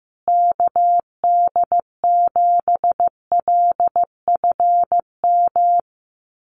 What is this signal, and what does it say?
Morse code, heard as a keyed tone, about 15 wpm, 700 Hz. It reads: KD7LFM
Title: USA Call Signs Repeated 2x 15wpm